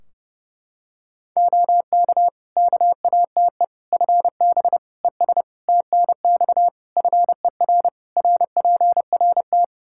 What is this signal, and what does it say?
Morse code, heard as a keyed tone, about 32 wpm, 700 Hz. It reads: OK KATE FB ES TNX FER RPRT
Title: Practice QSO - Exchange 2 - Info - 30wpm